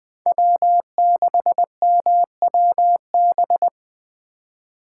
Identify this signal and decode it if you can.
Morse code, heard as a keyed tone, about 20 wpm, 700 Hz. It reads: W6MWB
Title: USA Call Signs - 5 Characters - 2x Character Spacing - 20wpm